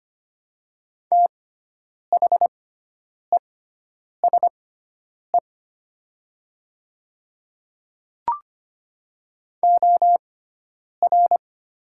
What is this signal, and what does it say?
Morse code, heard as a keyed tone, about 25 wpm, spaced as for 9 wpm, 700 Hz. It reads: THESE OR
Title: 6x Character Spacing - Top 300 Words - 25wpm